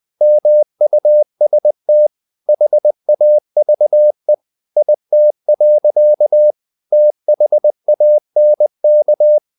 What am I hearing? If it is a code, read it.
Morse code, heard as a keyed tone, about 20 wpm, 600 Hz. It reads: MUST HAVE IT. THANK
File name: the old man and the sea / the old man and the sea part 1 20wpm